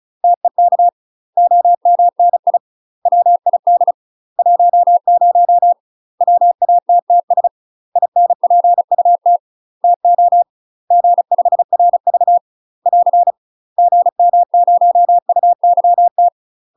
Morse code, heard as a keyed tone, about 34 wpm, 700 Hz. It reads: TEK OMNI WID 10 WATTS INPUT TO G5RV <AR> GM0UYT
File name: UK MASTER QSOs TRAINING / 35WPM / UK MASTER QSOs 0003